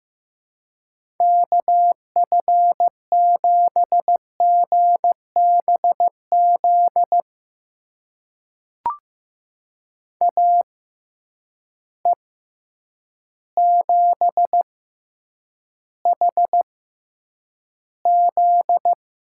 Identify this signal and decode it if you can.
Morse code, heard as a keyed tone, about 15 wpm, spaced as for 8 wpm, 700 Hz. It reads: KF7GBZ AE7HZ
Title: USA Call Signs - 6x Character Spacing - 15wpm